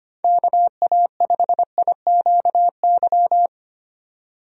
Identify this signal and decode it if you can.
Morse code, heard as a keyed tone, about 25 wpm, 700 Hz. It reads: KA5IQY
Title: USA Call Signs - 6 Characters - 25wpm